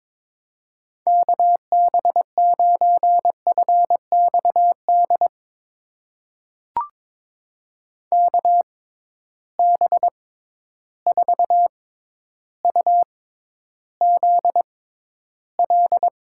Morse code, heard as a keyed tone, about 22 wpm, spaced as for 11 wpm, 700 Hz. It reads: KB9FXD KB4UZL
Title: USA Call Signs - 6 Characters - 6x Character Spacing - 22wpm